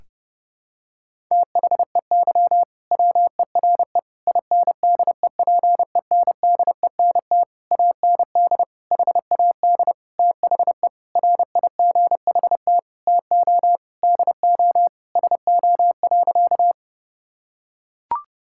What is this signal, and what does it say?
Morse code, heard as a keyed tone, about 30 wpm, 700 Hz. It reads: THEY WERE INDEPENDENT AND HAD THE RIGHT TO DO SO.
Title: Sentences from Top 1000 Words - Review Part 6 - 30wpm